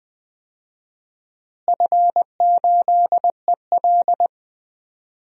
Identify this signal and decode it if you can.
Morse code, heard as a keyed tone, about 20 wpm, 700 Hz. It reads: F8EL
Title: DXCC-100 Call Signs - 4 Characters - Repeated 3x - 20wpm